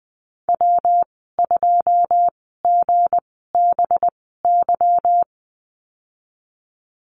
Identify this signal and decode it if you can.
Morse code, heard as a keyed tone, about 20 wpm, spaced as for 15 wpm, 700 Hz. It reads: W2GBY
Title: 186 Lesson 30g - Letter b - Call Signs 20wpm